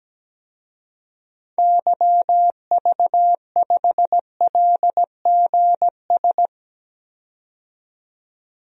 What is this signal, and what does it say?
Morse code, heard as a keyed tone, about 17 wpm, 700 Hz. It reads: YV5LGS
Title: DXCC-100 Call Signs - 6 Characters - 17wpm